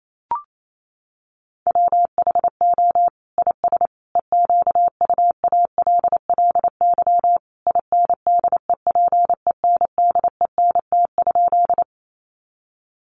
Morse code, heard as a keyed tone, about 28 wpm, 700 Hz. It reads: WHO IS EQUALLY INDEPENDENT?
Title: Sentences from Top 1400 Words 28wpm